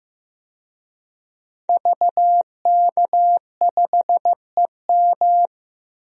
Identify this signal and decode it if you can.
Morse code, heard as a keyed tone, about 15 wpm, 700 Hz. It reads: VK5EM
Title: DXCC-100 Call Signs - 5 Characters - 15wpm